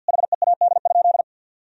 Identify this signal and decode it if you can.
Morse code, heard as a keyed tone, about 50 wpm, 700 Hz. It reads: HEAD?
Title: Sentences from Top 1000 Words - Review Part 3 - 50wpm